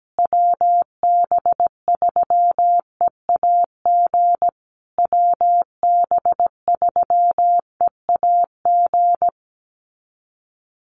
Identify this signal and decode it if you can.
Morse code, heard as a keyed tone, about 17 wpm, 700 Hz. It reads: WB3EAG WB3EAG
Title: USA Call Signs Repeated 3x 17wpm